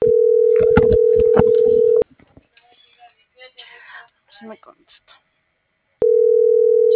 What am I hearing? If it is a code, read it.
Ringback tone (North America), 440 + 480 Hz.